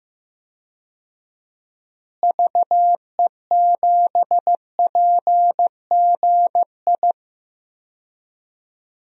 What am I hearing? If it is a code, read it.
Morse code, heard as a keyed tone, about 15 wpm, 700 Hz. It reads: VE7PGI
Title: DXCC-100 Call Signs - 6 Characters - Repeated 2x - 15wpm